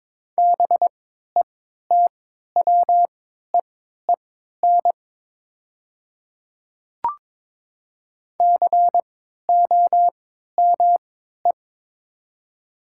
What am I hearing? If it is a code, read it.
Morse code, heard as a keyed tone, about 22 wpm, spaced as for 12 wpm, 700 Hz. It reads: BETWEEN COME
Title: 3x Character Spacing - Top 500 Words - 22wpm